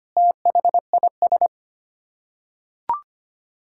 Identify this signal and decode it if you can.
Morse code, heard as a keyed tone, about 25 wpm, 700 Hz. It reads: THIS
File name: Sets of 9 Words - 25wpm